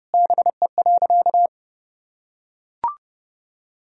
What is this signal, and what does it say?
Morse code, heard as a keyed tone, about 30 wpm, 700 Hz. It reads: BE.
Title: Sentences from Top 200 Words and Abbreviations 30wpm